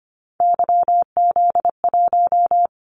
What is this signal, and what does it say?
Morse code, heard as a keyed tone, about 25 wpm, 700 Hz. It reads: YZ1
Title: Callsign Prefixes for Top 25 DXCC - 25wpm